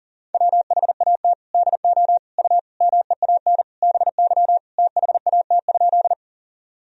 Morse code, heard as a keyed tone, about 40 wpm, 700 Hz. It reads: WHAT DO U MEAN BY THAT?
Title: Sentences from Top 200 Words and Abbreviations 40wpm